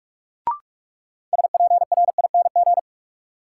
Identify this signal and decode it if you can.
Morse code, heard as a keyed tone, about 42 wpm, 700 Hz. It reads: SPRING
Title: Sentences from Top 1200 Words - New Word Drill Alt - 45wpm